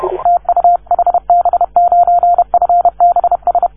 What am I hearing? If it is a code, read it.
Morse code, heard as a keyed tone, about 33 wpm, 690 Hz. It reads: TU HB9FBS
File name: 1esempio-telegrafia